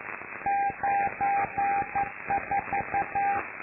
Morse code, heard as a keyed tone, about 11 wpm, 800 Hz, with noise. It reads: 94
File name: lu4aao / Globo Amsat